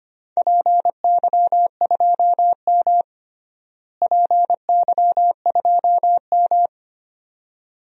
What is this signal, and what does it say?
Morse code, heard as a keyed tone, about 25 wpm, 700 Hz. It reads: PY2M PY2M
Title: DXCC-100 Call Signs - 4 Characters - Repeated 2x - 25wpm